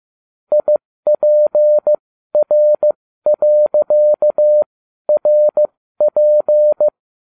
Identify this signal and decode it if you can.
Morse code, heard as a keyed tone, about 15 wpm, spaced as for 12 wpm, 600 Hz. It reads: IPR.RP